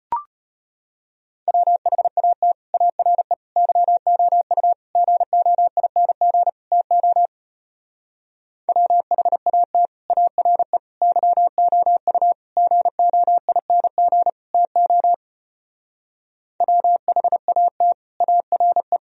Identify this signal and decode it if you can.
Morse code, heard as a keyed tone, about 33 wpm, 700 Hz. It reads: WHAT ARE YOU GOING TO WHAT ARE YOU GOING TO WHAT ARE